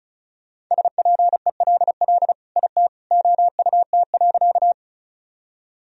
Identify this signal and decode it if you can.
Morse code, heard as a keyed tone, about 34 wpm, 700 Hz. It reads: SPELL IT OUT.
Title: Sentences from Top 1000 Words - Review Part 1 - 35wpm